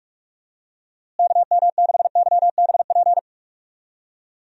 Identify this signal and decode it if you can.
Morse code, heard as a keyed tone, about 43 wpm, 700 Hz. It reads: KM6YBP
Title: USA Call Signs - 6 Characters - Repeated 3x - 45wpm